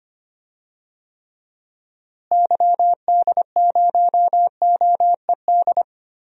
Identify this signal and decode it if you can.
Morse code, heard as a keyed tone, about 25 wpm, 700 Hz. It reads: YD0OED
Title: DXCC-100 Call Signs - 6 Characters - Repeated 3x - 25wpm